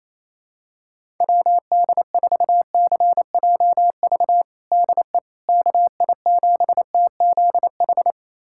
Morse code, heard as a keyed tone, about 28 wpm, 700 Hz. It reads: WD4CJV DE KI7TZH
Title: USA Callsign DE Callsign - 6 Characters - 28wpm